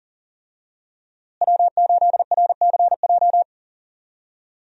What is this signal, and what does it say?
Morse code, heard as a keyed tone, about 37 wpm, 700 Hz. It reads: W8RCJ
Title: USA Call Signs - 5 Characters - Repeated 3x - 40wpm